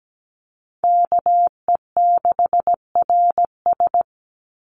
Morse code, heard as a keyed tone, about 17 wpm, 700 Hz. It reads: KE6RS
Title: NA SOTA Chaser Callsigns 17wpm